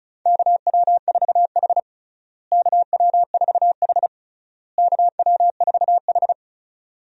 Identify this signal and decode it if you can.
Morse code, heard as a keyed tone, about 36 wpm, 700 Hz. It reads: KW4H KW4H KW4H
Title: USA Call Signs - 4 Characters - Repeated 3x - 35wpm